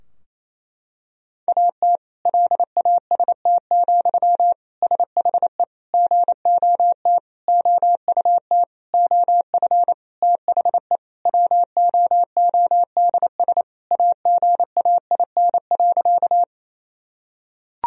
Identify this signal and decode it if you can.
Morse code, heard as a keyed tone, about 28 wpm, 700 Hz. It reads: AT LAST, SHE GOT OUT OF THE WOODS AGAIN.
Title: Sentences from Top 1000 Words - Review Part 2 - 28wpm